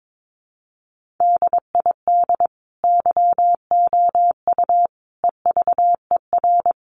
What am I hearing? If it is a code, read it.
Morse code, heard as a keyed tone, about 22 wpm, 700 Hz. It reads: DID YOU EVER
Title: Sentences from Top 1400 Words 22wpm